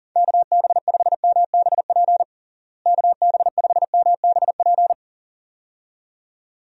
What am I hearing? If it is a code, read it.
Morse code, heard as a keyed tone, about 40 wpm, 700 Hz. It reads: KB5MBP KB5MBP
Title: USA Call Signs - 6 Characters - Repeated 2x - 40wpm